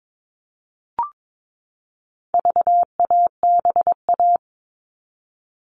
Morse code, heard as a keyed tone, about 22 wpm, 700 Hz. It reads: VABA
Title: All Cut Numbers - 4-Digit 22wpm